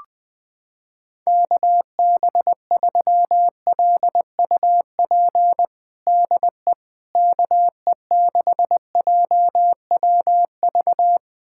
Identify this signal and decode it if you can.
Morse code, heard as a keyed tone, about 20 wpm, 700 Hz. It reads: KB3LUP DE KE6JWV
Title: USA Callsign DE Callsign 20wpm